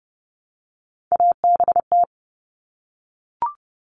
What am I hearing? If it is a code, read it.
Morse code, heard as a keyed tone, about 30 wpm, 700 Hz. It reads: ABT
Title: Abbreviations Repeated 3x - 30wpm